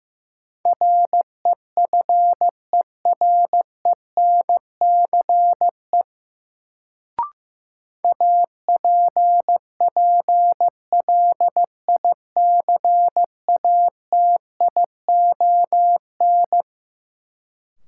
Morse code, heard as a keyed tone, about 15 wpm, 700 Hz. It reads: REFERENCE APPLICATION
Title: Sentences from Top 1200 Words - New Word Drill - 15wpm